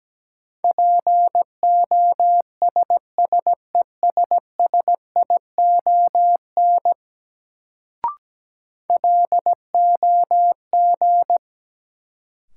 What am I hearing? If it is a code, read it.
Morse code, heard as a keyed tone, about 17 wpm, 700 Hz. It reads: POSSESSION LOG
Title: Sentences from Top 1400 Words - New Word Drill - 17wpm